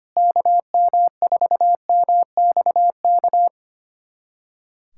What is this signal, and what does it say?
Morse code, heard as a keyed tone, about 25 wpm, 700 Hz. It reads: KM4MXK
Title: USA Callsign DE Callsign - 6 Characters - 25wpm